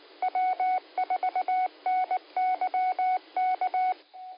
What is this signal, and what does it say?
Morse code, heard as a keyed tone, about 19 wpm, 2200 Hz, with noise. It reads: W4NYK
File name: qsorec SimplexLogic